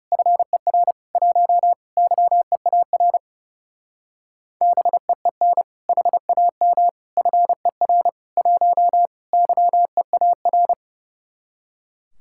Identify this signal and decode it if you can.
Morse code, heard as a keyed tone, about 38 wpm, 700 Hz. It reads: FER 1 YEAR BEEN HAM FER 1 YEAR